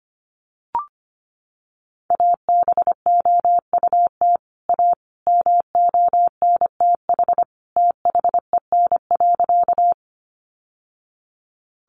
Morse code, heard as a keyed tone, about 25 wpm, 700 Hz. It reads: ABOUT A MONTH THEN.
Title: Sentences from Top 800 Words 25wpm